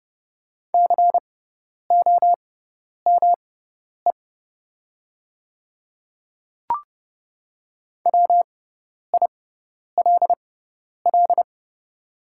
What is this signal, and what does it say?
Morse code, heard as a keyed tone, about 30 wpm, spaced as for 10 wpm, 700 Hz. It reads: COME WILL
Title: 6x Character Spacing - Top 100 Words - 30wpm